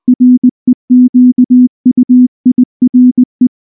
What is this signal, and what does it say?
Morse code, heard as a keyed tone, about 19 wpm, 250 Hz. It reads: REQUIRE
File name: morse / morse-training-20WPM-2